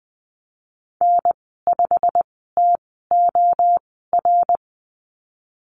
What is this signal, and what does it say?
Morse code, heard as a keyed tone, about 20 wpm, spaced as for 15 wpm, 700 Hz. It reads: N5TOR
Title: 122 Lesson 20g - Number 3 - Call Signs 20wpm